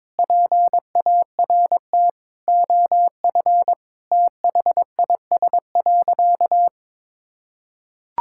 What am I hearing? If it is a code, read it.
Morse code, heard as a keyed tone, about 22 wpm, 700 Hz. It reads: PART OF THIS.
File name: practice / 22wpm / Sentences with 3-Word Phrases - 22wpm